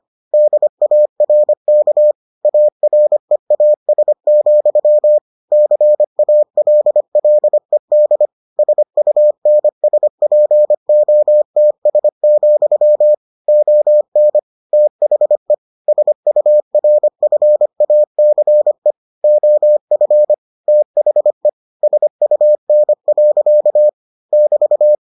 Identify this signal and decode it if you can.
Morse code, heard as a keyed tone, about 25 wpm, 600 Hz. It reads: DARK AREAS, CALLED SUNSPOTS, ON THE SURFACE OF THE SUN. <BT>